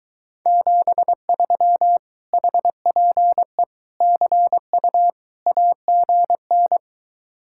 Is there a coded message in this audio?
Yes: Morse code from 0.5 to 6.8 s, heard as a keyed tone, about 23 wpm, 700 Hz. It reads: 73 HPE CU AGN